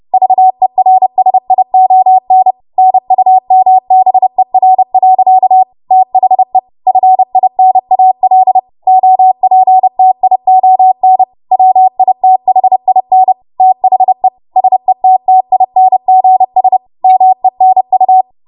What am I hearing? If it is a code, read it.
Morse code, heard as a keyed tone, about 30 wpm, 750 Hz. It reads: VERSION NUMBER. THE FINAL OPTION WITHIN THE SETTINGS MENU
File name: Morse / 30WPM